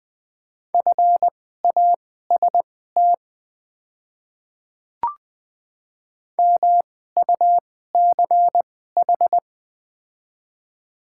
Morse code, heard as a keyed tone, about 20 wpm, spaced as for 15 wpm, 690 Hz. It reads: FAST MUCH
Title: 2x Character Spacing - Top 500 Words - 20wpm